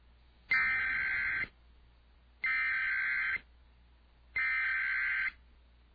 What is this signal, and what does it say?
Emergency Alert System (SAME) tones (header bursts), heard as harsh buzzing bursts. It is heard from 0.5 to 5.3 s.